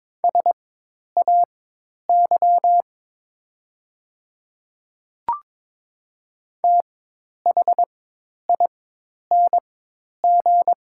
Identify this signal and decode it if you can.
Morse code, heard as a keyed tone, about 22 wpm, spaced as for 10 wpm, 690 Hz. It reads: SAY THING
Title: 4x Character Spacing - Top 300 Words - 22wpm